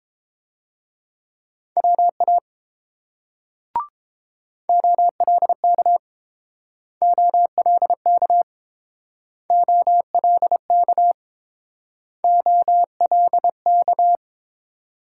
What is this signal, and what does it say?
Morse code, heard as a keyed tone, about 30 wpm, 700 Hz. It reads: WA OLK OLK OLK OLK